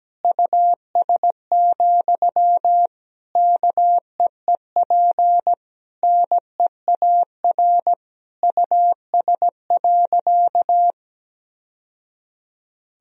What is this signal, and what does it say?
Morse code, heard as a keyed tone, about 17 wpm, 700 Hz. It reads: US, KEEP NEAR US.
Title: Sentences from Top 1000 Words - Review Part 2 - 17wpm